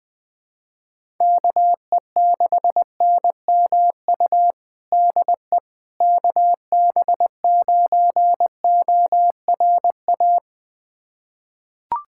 Morse code, heard as a keyed tone, about 19 wpm, 700 Hz. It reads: KE6NMU DE KB9ORA
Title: USA Callsign DE Callsign 20wpm